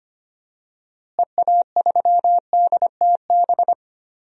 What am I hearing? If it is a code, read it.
Morse code, heard as a keyed tone, about 25 wpm, 700 Hz. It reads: EA3DTB
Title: DXCC-100 Call Signs - 6 Characters - Repeated 2x - 25wpm